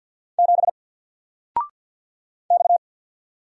Morse code, heard as a keyed tone, about 50 wpm, 700 Hz. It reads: 7 X
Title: Single Letter-Number Alt 50wpm